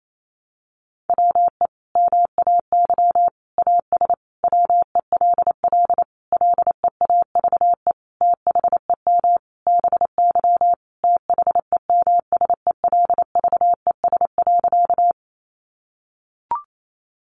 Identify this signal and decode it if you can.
Morse code, heard as a keyed tone, about 28 wpm, 700 Hz. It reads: WE MAY AS WELL LEAVE THEM BY THEMSELVES.
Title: Sentences from Top 800 Words 28wpm